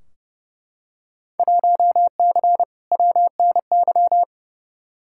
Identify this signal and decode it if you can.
Morse code, heard as a keyed tone, about 29 wpm, 700 Hz. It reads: 1C WNY
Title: Field Day Exchanges Encore - 30wpm